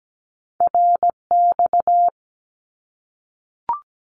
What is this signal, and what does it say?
Morse code, heard as a keyed tone, about 17 wpm, 700 Hz. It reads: RX
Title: Abbreviations Repeated 3x - 17wpm